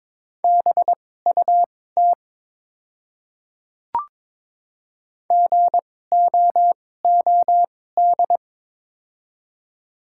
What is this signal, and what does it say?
Morse code, heard as a keyed tone, about 22 wpm, spaced as for 16 wpm, 700 Hz. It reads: BUT GOOD
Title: 2x Character Spacing - Top 300 Words - 22wpm